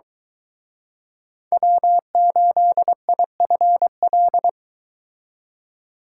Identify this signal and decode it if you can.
Morse code, heard as a keyed tone, about 23 wpm, 700 Hz. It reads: W8IFL